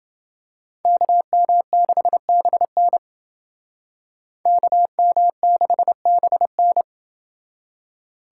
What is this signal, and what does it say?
Morse code, heard as a keyed tone, about 29 wpm, 700 Hz. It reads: KM6BN KM6BN